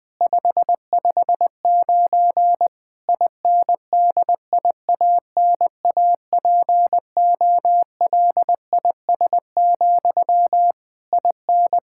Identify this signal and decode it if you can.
Morse code, heard as a keyed tone, about 20 wpm, 700 Hz. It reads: 559 INDIANAPOLIS, IN
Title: Practice QSO - Exchange 1 - Beginning - 20wpm